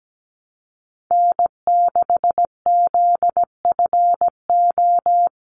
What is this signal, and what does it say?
Morse code, heard as a keyed tone, about 17 wpm, 700 Hz. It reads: N6ZFO
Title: Call Signs Repeated 3x - 17wpm